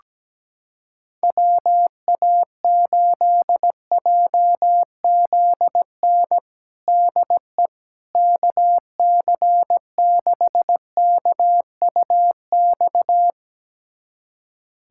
Morse code, heard as a keyed tone, about 17 wpm, 700 Hz. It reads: WA8JZN DE KC6KUX
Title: USA Callsign DE Callsign - 6 Characters - 17wpm